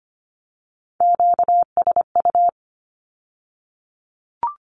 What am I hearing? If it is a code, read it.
Morse code, heard as a keyed tone, about 25 wpm, 700 Hz. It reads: QSU
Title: Q-Codes with Spoken Meaning - 25wpm